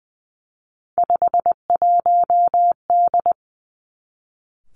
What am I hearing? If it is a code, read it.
Morse code, heard as a keyed tone, about 20 wpm, 700 Hz. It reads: 51D
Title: Three Challenging Letter-Number Combinations 20wpm